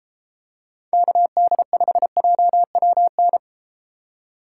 Morse code, heard as a keyed tone, about 32 wpm, 700 Hz. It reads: KD5JWN